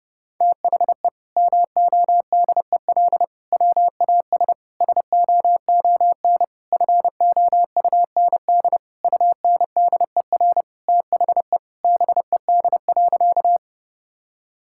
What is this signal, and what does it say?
Morse code, heard as a keyed tone, about 32 wpm, 700 Hz. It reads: THE MODEL WAS SOON FOUND UNDER THE BED.
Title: Sentences from Top 1000 Words - Review Part 5 - 30wpm